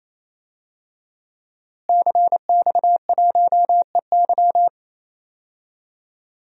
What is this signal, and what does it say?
Morse code, heard as a keyed tone, about 27 wpm, 700 Hz. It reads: CX1EY
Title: DXCC-100 Call Signs - 5 Characters - Repeated 3x - 28wpm